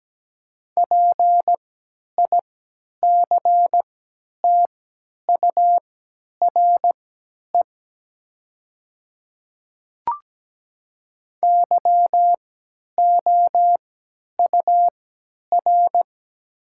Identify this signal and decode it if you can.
Morse code, heard as a keyed tone, about 17 wpm, spaced as for 10 wpm, 700 Hz. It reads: PICTURE YOUR
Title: 3x Character Spacing - Top 300 Words - 17wpm